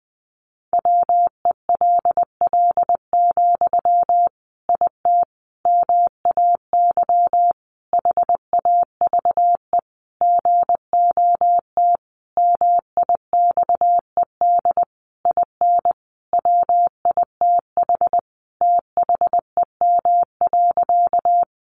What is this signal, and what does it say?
Morse code, heard as a keyed tone, about 20 wpm, 700 Hz. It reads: WELL, IT MAY HAVE GOT MIXED IN WITH THEM.
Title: Sentences from Top 1500 Words 20wpm